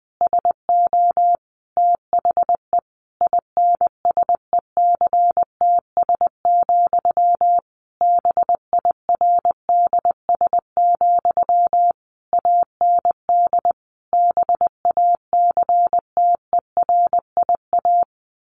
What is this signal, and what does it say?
Morse code, heard as a keyed tone, about 20 wpm, 700 Hz. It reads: SO THE INSECTS, BIRDS, AND BACTERIA
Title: A Sound Of Thunder 20wpm